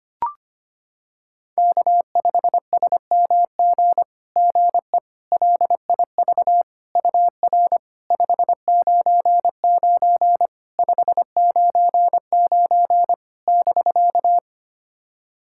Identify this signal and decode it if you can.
Morse code, heard as a keyed tone, about 25 wpm, 700 Hz. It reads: K5SMG GE LIV UR 599 599 <BK>
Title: SOTA QSO Practice 25wpm